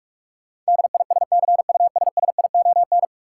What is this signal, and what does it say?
Morse code, heard as a keyed tone, about 46 wpm, 700 Hz. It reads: DISCUSSION
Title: Sentences from Top 1100 Words - New Word Drill - 45wpm